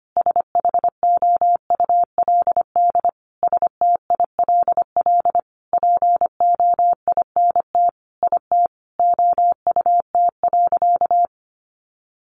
Morse code, heard as a keyed tone, about 25 wpm, 700 Hz. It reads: SHOULD STILL POINT IT OUT.
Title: Sentences from Top 1500 Words 25wpm